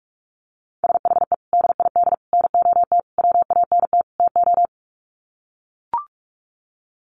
Morse code, heard as a keyed tone, about 44 wpm, 700 Hz. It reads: SHE DID NOT WANT TO
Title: Sets of 5 Words 45wpm